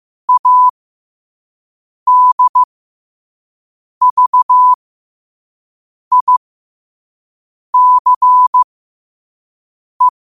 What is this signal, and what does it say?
Morse code, heard as a keyed tone, about 15 wpm, spaced as for 5 wpm, 1000 Hz. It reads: ADVICE
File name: qotdfiles / 5wpm